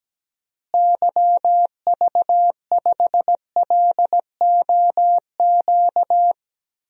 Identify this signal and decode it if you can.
Morse code, heard as a keyed tone, about 17 wpm, 700 Hz. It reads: YV5LOQ
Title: DXCC-100 Call Signs - 6 Characters - Repeated 2x - 17wpm